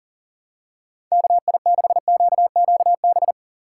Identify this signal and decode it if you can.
Morse code, heard as a keyed tone, about 40 wpm, 700 Hz. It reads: KI6QQB
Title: USA Call Signs - 6 Characters - 4x Character Spacing - 40wpm